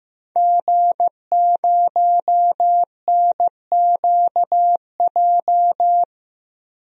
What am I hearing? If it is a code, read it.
Morse code, heard as a keyed tone, about 15 wpm, 700 Hz. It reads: G0NQJ
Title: DXCC-100 Call Signs - 5 Characters - 15wpm